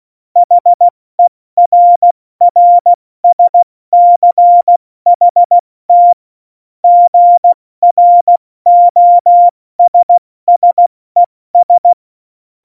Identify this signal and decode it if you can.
Morse code, heard as a keyed tone, about 16 wpm, 700 Hz. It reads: HERRSCHT GROSSES
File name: Morsekurs / Witze / Witz 17. November 2020, 70 F80